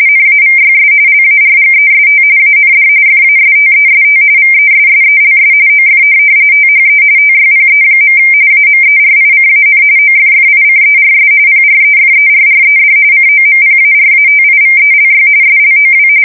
RTTY, about 45.45 baud. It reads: NAVYS LARGEST AND FASTEST VESSELS OF THE TYPE. AMERICAN OFFICERS PREFERRED COMPARATIVELY SLOW
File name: fsk170 / USS-Missouri